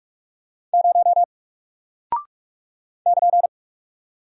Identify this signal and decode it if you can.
Morse code, heard as a keyed tone, about 44 wpm, 700 Hz. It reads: 0 <KN>
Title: Single Character (Letter:Number:Punctuation:Prosign) 45wpm